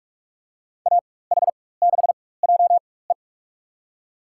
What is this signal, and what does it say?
Morse code, heard as a keyed tone, about 45 wpm, spaced as for 21 wpm, 700 Hz. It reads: AH6JE
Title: USA Call Signs - 5 Characters - 4x Character Spacing - 45wpm